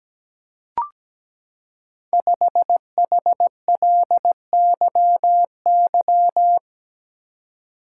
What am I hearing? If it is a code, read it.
Morse code, heard as a keyed tone, about 17 wpm, 700 Hz. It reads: 5HLYY
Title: Code Groups - 5 Letter:Numbers - 17wpm